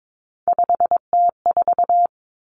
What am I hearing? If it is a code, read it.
Morse code, heard as a keyed tone, about 22 wpm, 700 Hz. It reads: 5T4
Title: Three Challenging Letter-Number Combinations 22wpm